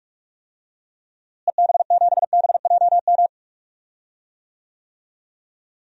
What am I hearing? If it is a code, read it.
Morse code, heard as a keyed tone, about 46 wpm, 700 Hz. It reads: EB7BJM
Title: DXCC-100 Call Signs - 6 Characters - Repeated 3x - 45wpm